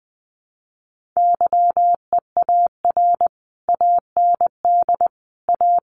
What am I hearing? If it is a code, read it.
Morse code, heard as a keyed tone, about 20 wpm, 700 Hz. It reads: YEAR AND A
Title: Sets of 4 Words 20wpm